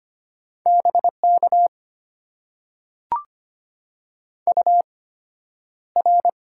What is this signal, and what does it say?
Morse code, heard as a keyed tone, about 25 wpm, spaced as for 11 wpm, 690 Hz. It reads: BK UR
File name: Top 90 QSO Elements - 8x Character Spacing - 25wpm